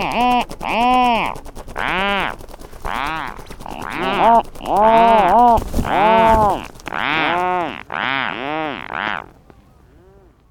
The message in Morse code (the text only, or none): none